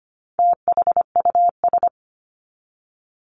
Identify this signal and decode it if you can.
Morse code, heard as a keyed tone, about 25 wpm, 700 Hz. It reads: THUS
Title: Top 751-1000 Words 25wpm